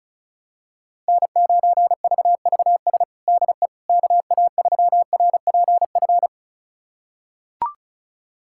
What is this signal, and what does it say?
Morse code, heard as a keyed tone, about 36 wpm, 700 Hz. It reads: N9VVS DE KA3RPF
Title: USA Callsign DE Callsign 35wpm